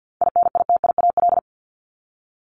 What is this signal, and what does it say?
Morse code, heard as a keyed tone, about 50 wpm, 700 Hz. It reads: INITIAL
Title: Top 1501-1750 Words 50wpm